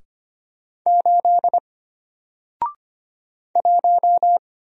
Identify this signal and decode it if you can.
Morse code, heard as a keyed tone, about 25 wpm, 700 Hz. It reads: 8 1
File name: Single-Letter-Number-25wpm